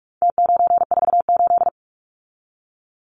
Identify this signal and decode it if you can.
Morse code, heard as a keyed tone, about 44 wpm, 700 Hz. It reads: T948
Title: Numbers - Contest Style - 4-Digit 45wpm